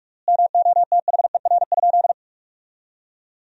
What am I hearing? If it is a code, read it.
Morse code, heard as a keyed tone, about 46 wpm, 700 Hz. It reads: MOTHER?
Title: 3x Word-Spacing - Sentences from Top 200 Words - 45wpm